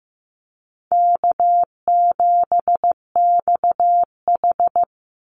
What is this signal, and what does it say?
Morse code, heard as a keyed tone, about 15 wpm, 700 Hz. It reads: K7XH
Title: Call Signs Repeated 2x - 15wpm